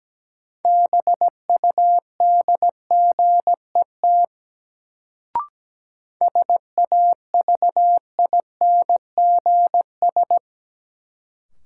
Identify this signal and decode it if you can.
Morse code, heard as a keyed tone, about 17 wpm, 700 Hz. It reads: BUDGET SAVINGS
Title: Sentences from Top 1200 Words - New Word Drill - 17wpm